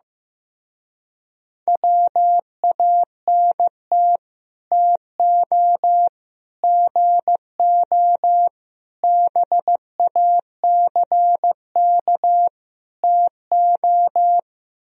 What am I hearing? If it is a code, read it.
Morse code, heard as a keyed tone, about 15 wpm, 700 Hz. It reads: WANT TO GO BACK TO
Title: Speed-Racing - Sets of 5 Words 15wpm